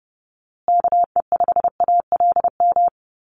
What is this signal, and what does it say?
Morse code, heard as a keyed tone, about 30 wpm, 700 Hz. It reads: KE5ALM
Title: USA Call Signs Repeated 3x 30wpm